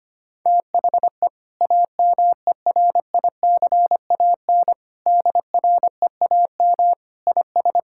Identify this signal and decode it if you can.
Morse code, heard as a keyed tone, about 25 wpm, 700 Hz. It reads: THE AMERICAN DREAM IS
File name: practice / 25wpm / Sentences with 2-Word Phrases - 25wpm